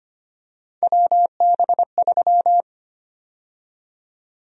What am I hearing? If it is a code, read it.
Morse code, heard as a keyed tone, about 25 wpm, 700 Hz. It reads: WB3
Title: USA Call Sign Prefixes - Repeated 2x - 25wpm